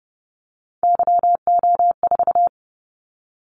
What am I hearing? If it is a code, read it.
Morse code, heard as a keyed tone, about 31 wpm, 700 Hz. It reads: YO4
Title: Callsign Prefixes for Top 25 DXCC - 30wpm